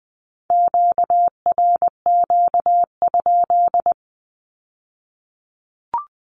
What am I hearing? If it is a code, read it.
Morse code, heard as a keyed tone, about 20 wpm, 700 Hz. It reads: QRQ?
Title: Q-Codes with Spoken Meaning - 20wpm